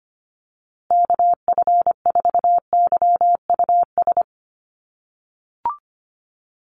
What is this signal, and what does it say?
Morse code, heard as a keyed tone, about 25 wpm, 700 Hz. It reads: KF4YUS
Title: USA Call Signs Repeated 3x 25wpm